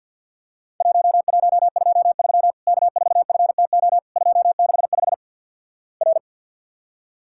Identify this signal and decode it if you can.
Morse code, heard as a keyed tone, about 52 wpm, 700 Hz. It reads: 1123 K4FTO 265 R
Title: Worked All Europe DX Contest - QTC Exchange with Repeats - 50wpm